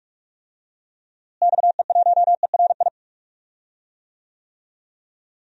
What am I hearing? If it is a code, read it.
Morse code, heard as a keyed tone, about 46 wpm, 700 Hz. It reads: XE1ERI
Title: DXCC-100 Call Signs - 6 Characters - Repeated 3x - 45wpm